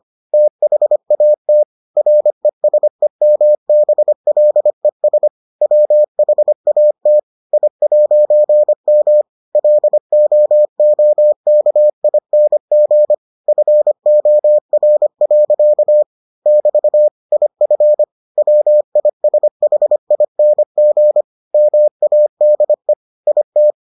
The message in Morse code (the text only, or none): THAT RESEMBLES WHAT I'M LOOKING FOR. <BT> IF WISHING MADE IT